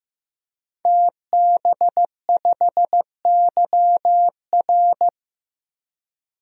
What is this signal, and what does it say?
Morse code, heard as a keyed tone, about 15 wpm, 700 Hz. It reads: TB5YR
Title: DXCC-100 Call Signs - 5 Characters - 15wpm